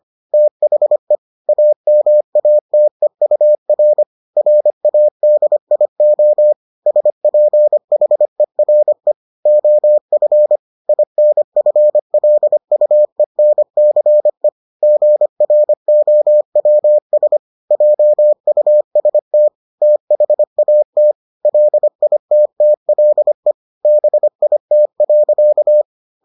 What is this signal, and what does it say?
Morse code, heard as a keyed tone, about 25 wpm, 600 Hz. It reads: THE AMATEUR RADIO SPHERE OF INFLUENCE GROWS JUST THAT LITTLE BIT.